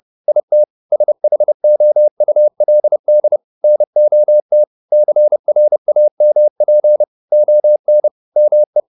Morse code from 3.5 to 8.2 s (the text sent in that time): NOT CRAMP ON